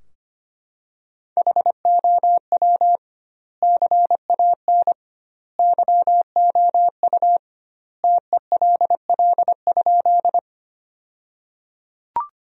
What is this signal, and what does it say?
Morse code, heard as a keyed tone, about 25 wpm, 700 Hz. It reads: HOW CAN YOU TELL?
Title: 2x Word-Spacing - Sentences from Top 200 Words - 25wpm